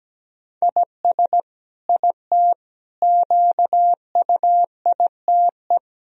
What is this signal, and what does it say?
Morse code, heard as a keyed tone, about 17 wpm, 700 Hz. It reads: IS IT QUITE
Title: Sentences from Top 1000 Words - Review Part 3 - 17wpm